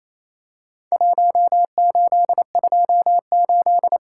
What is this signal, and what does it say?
Morse code, heard as a keyed tone, about 27 wpm, 700 Hz. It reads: 1828
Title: Worked All Europe DX Contest - QTC Exchange - 28wpm